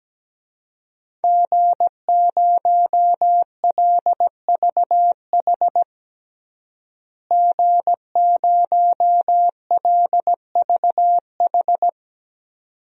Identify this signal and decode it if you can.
Morse code, heard as a keyed tone, about 17 wpm, 700 Hz. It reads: G0LVH G0LVH
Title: DXCC-100 Call Signs - 5 Characters - Repeated 2x - 17wpm